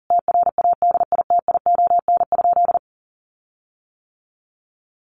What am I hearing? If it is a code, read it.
Morse code, heard as a keyed tone, about 40 wpm, 700 Hz. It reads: TRADITION?
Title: Sentences from Top 1500 Words 40wpm